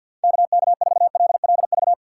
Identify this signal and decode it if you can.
Morse code, heard as a keyed tone, about 52 wpm, 700 Hz. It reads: KK4LLV
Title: USA Call Signs - 6 Characters - 8x Character Spacing - 50wpm